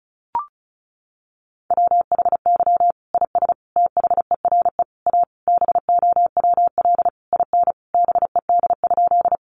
Morse code, heard as a keyed tone, about 34 wpm, 700 Hz. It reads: WHY IS THERE A BOWL IN BED?
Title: Sentences from Top 1200 Words 35wpm